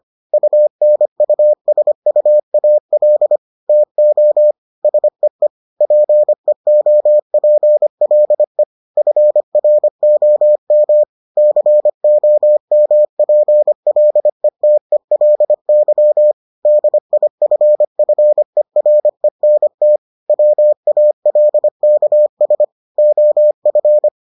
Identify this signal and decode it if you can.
Morse code, heard as a keyed tone, about 25 wpm, 600 Hz. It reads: UNUSUAL TO SEE PEOPLE FROM COMPLETELY DIFFERENT WALKS OF